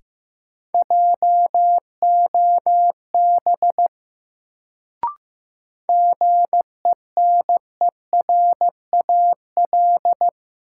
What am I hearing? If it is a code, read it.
Morse code, heard as a keyed tone, about 15 wpm, 700 Hz. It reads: JOB GENERAL
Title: Sentences from Top 600 Words - New Word Drill - 15wpm